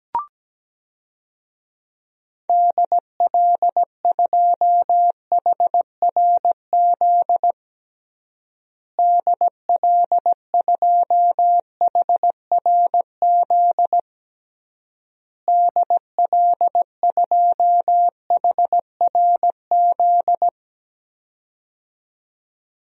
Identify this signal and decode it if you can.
Morse code, heard as a keyed tone, about 17 wpm, 700 Hz. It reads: DL2HRZ DL2HRZ DL2HRZ
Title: DXCC-100 Call Signs - 6 Characters - Repeated 3x - 17wpm